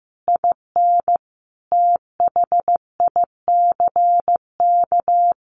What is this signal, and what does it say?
Morse code, heard as a keyed tone, about 15 wpm, 700 Hz. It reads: IN THICK
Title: A Sound Of Thunder 15wpm